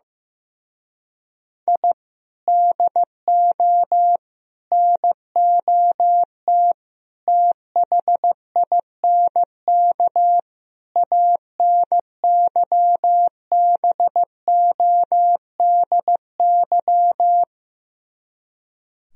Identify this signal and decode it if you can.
Morse code, heard as a keyed tone, about 15 wpm, 700 Hz. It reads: I DO NOT THINK ANYBODY
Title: Speed-Racing - Sets of 5 Words 15wpm